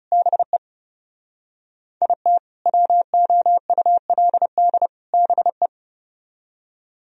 Morse code, heard as a keyed tone, about 39 wpm, 700 Hz. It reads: BE IT WOULD BE